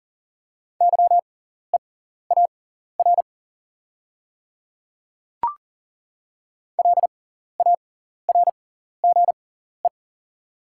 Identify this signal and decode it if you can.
Morse code, heard as a keyed tone, about 40 wpm, spaced as for 14 wpm, 700 Hz. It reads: YEAR LARGE
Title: 6x Character Spacing - Top 200 Words - 40wpm